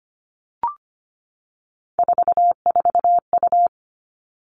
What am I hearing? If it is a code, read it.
Morse code, heard as a keyed tone, about 25 wpm, 700 Hz. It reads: 44U
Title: All Cut Numbers - 3-Digit 25wpm